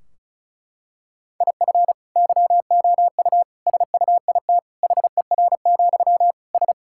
Morse code, heard as a keyed tone, about 36 wpm, 700 Hz. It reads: IF YOU SUIT HER, S
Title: Sentences from Top 1600 Words 35wpm